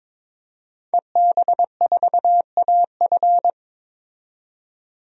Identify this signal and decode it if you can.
Morse code, heard as a keyed tone, about 22 wpm, 700 Hz. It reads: EB4AF
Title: DXCC-100 Call Signs - 5 Characters - 22wpm